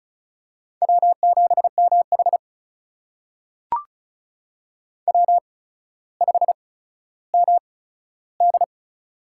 Morse code, heard as a keyed tone, about 36 wpm, spaced as for 15 wpm, 700 Hz. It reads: W7MH W5MD
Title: USA Call Signs - 4 Characters - 8x Character Spacing - 35wpm